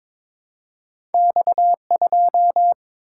Morse code, heard as a keyed tone, about 22 wpm, 700 Hz. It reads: X2
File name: practice / 22wpm / Two Letter-Number Combinations - 22wpm